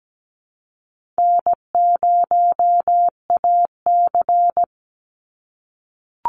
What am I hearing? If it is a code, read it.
Morse code, heard as a keyed tone, about 17 wpm, 700 Hz. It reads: N0AC
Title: Easy Call Signs - 17wpm